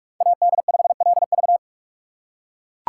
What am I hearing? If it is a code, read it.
Morse code, heard as a keyed tone, about 42 wpm, 700 Hz. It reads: AD5LV